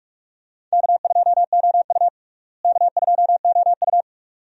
Morse code, heard as a keyed tone, about 46 wpm, 700 Hz. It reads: K2OU K2OU
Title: USA Call Signs - 4 Characters - Repeated 2x - 45wpm